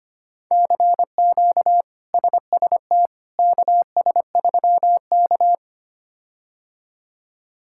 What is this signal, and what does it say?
Morse code, heard as a keyed tone, about 25 wpm, 700 Hz. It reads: CQ SST KS3K
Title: K1USN SST QSO Practice - 25wpm